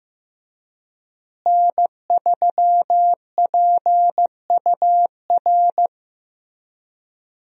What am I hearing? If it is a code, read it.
Morse code, heard as a keyed tone, about 15 wpm, 700 Hz. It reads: N3PUR
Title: USA Call Signs - 5 Characters - Repeated 3x - 15wpm